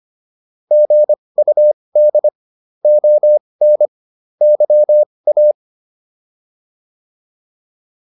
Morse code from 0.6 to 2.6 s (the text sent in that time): GUD